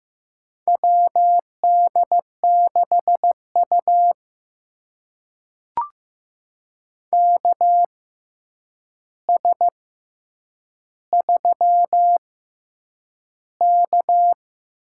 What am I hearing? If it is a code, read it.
Morse code, heard as a keyed tone, about 15 wpm, spaced as for 8 wpm, 700 Hz. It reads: WD6U KS3K
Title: USA Call Signs - 4 Characters - 6x Character Spacing - 15wpm